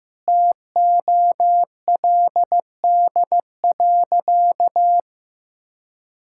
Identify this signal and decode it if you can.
Morse code, heard as a keyed tone, about 15 wpm, 700 Hz. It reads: TOLD.
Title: Sentences from Top 1000 Words - Review Part 6 - 15wpm